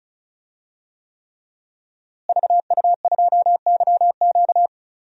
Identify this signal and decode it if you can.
Morse code, heard as a keyed tone, about 36 wpm, 700 Hz. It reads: VU2YQ
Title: DXCC-100 Call Signs - 5 Characters - Repeated 3x - 35wpm